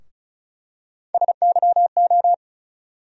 Morse code, heard as a keyed tone, about 36 wpm, 700 Hz. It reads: SYO
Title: USA Call Sign Postfixes - 3 Letters 35wpm